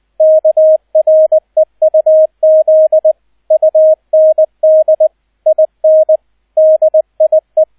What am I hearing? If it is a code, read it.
Morse code, heard as a keyed tone, about 20 wpm, 630 Hz, with radio noise. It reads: KREUZ UND IN DIE
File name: Erster Streich 20wpm